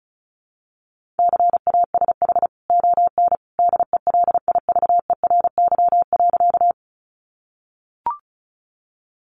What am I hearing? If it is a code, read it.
Morse code, heard as a keyed tone, about 34 wpm, 700 Hz. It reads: CASH ON DELIVERY.
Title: Sentences from Top 1500 Words 35wpm